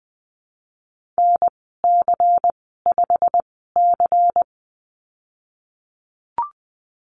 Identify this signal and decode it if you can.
Morse code, heard as a keyed tone, about 20 wpm, spaced as for 15 wpm, 700 Hz. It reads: NC5C
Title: 167 Lesson 27g - Number 7 - Call Signs 20wpm